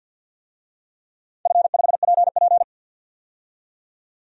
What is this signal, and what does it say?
Morse code, heard as a keyed tone, about 47 wpm, 700 Hz. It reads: F5PP
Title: DXCC-100 Call Signs - 4 Characters - 50wpm